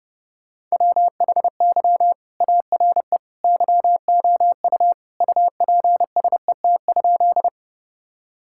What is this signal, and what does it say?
Morse code, heard as a keyed tone, about 32 wpm, 700 Hz. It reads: WHY ARE YOU UPSET?
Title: Sentences from Top 1000 Words - Review Part 5 - 30wpm